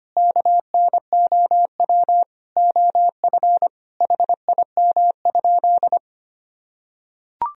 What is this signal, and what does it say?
Morse code, heard as a keyed tone, about 25 wpm, 700 Hz. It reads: KNOW OF HIM?
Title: Sentences from Top 100 Words and Abbreviations 25wpm